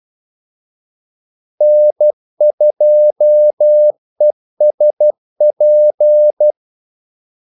Morse code, heard as a keyed tone, about 12 wpm, 600 Hz. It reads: N2ESP
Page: 2 ekstra ordmellemrum mellem kaldesignalerne